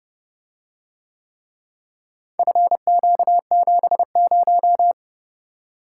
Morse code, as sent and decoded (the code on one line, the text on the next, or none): ..-. --.- --... -----
FQ70